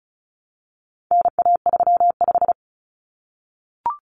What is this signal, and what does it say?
Morse code, heard as a keyed tone, about 34 wpm, 700 Hz. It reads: NA35
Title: Common Cut Numbers - 4-Digit 35wpm